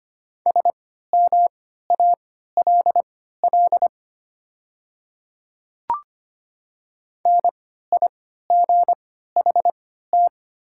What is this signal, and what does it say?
Morse code, heard as a keyed tone, about 25 wpm, spaced as for 14 wpm, 700 Hz. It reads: SMALL NIGHT
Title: 3x Character Spacing - Top 300 Words - 25wpm